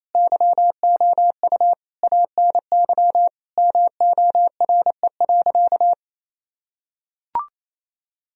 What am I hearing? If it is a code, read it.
Morse code, heard as a keyed tone, about 27 wpm, 700 Hz. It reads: YOU ANY MORE.
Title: Sentences from Top 1000 Words - Review Part 1 - 28wpm